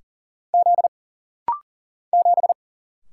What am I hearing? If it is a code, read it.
Morse code, heard as a keyed tone, about 40 wpm, 700 Hz. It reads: Z 7
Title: Single Letter-Number Alt - Rapid-Fire 40wpm